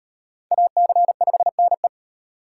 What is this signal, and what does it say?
Morse code, heard as a keyed tone, about 36 wpm, 700 Hz. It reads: AC5NE